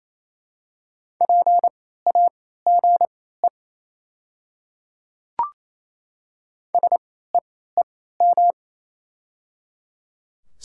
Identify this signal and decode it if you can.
Morse code, heard as a keyed tone, about 28 wpm, spaced as for 16 wpm, 700 Hz. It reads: PAGE SEEM
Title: 3x Character Spacing - Top 300 Words - 28wpm